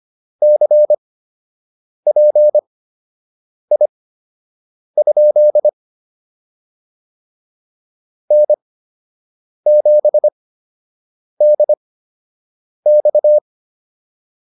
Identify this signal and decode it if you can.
Morse code, heard as a keyed tone, about 25 wpm, spaced as for 7 wpm, 600 Hz. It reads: CPI? N7DX